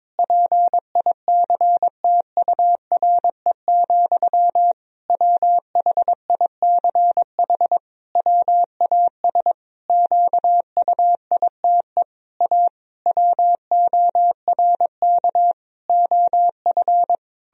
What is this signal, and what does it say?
Morse code, heard as a keyed tone, about 22 wpm, 700 Hz. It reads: PICTURE, WHICH WAS QUITE A WORK OF
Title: Sentences from Top 1000 Words - Review Part 3 - 22wpm